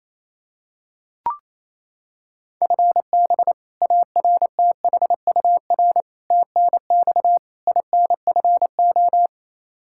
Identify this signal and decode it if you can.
Morse code, heard as a keyed tone, about 28 wpm, 700 Hz. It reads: FB ARTHUR TNX INFO
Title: Practice QSO - Exchange 3 - More Info - 28wpm